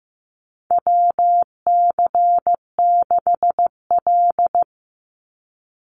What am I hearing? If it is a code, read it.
Morse code, heard as a keyed tone, about 15 wpm, 700 Hz. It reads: WC6L
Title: USA Call Signs 15wpm